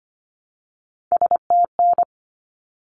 Morse code, heard as a keyed tone, about 25 wpm, 700 Hz. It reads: STN
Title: Abbreviations 25wpm